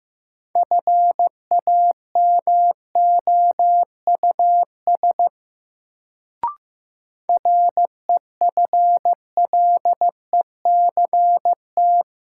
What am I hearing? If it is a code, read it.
Morse code, heard as a keyed tone, about 16 wpm, 700 Hz. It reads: FAMOUS REFLECT
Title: Sentences from Top 1500 Words - New Word Drill - 15wpm